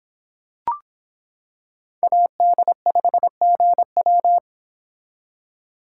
Morse code, heard as a keyed tone, about 26 wpm, 700 Hz. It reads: AD5GW